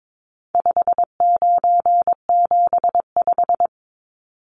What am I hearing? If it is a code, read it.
Morse code, heard as a keyed tone, about 22 wpm, 700 Hz. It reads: 5975
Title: Numbers - 4-Digit 22wpm